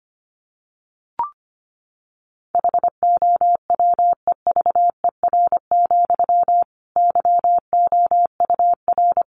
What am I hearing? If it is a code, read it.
Morse code, heard as a keyed tone, about 25 wpm, 700 Hz. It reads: HOWEVER, YOUR
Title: Sentences from Top 1200 Words 25wpm